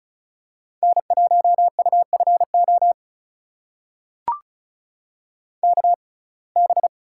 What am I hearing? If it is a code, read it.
Morse code, heard as a keyed tone, about 36 wpm, spaced as for 25 wpm, 700 Hz. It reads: N1UFO KB
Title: USA Call Signs - 6x Character Spacing - 35wpm